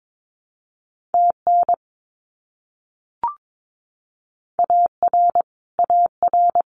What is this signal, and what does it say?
Morse code, heard as a keyed tone, about 22 wpm, 700 Hz. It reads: TN AR AR
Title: US States Repeated 3x 22wpm